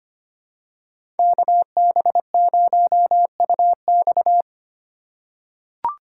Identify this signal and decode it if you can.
Morse code, heard as a keyed tone, about 25 wpm, 700 Hz. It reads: KB0UX
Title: USA Call Signs - 5 Characters - 3x Character Spacing - 25wpm